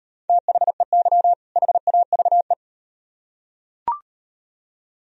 Morse code, heard as a keyed tone, about 38 wpm, 700 Hz. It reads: THEY HAVE